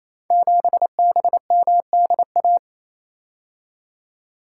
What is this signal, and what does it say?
Morse code, heard as a keyed tone, about 27 wpm, 700 Hz. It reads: 7BMDA
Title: Code Groups - 5 Letter:Numbers - 28wpm